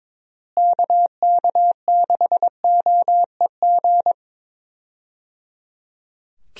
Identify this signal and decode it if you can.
Morse code, heard as a keyed tone, about 22 wpm, 700 Hz. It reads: KK6OEG
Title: USA Call Signs - 6 Characters - Repeated 3x - 22wpm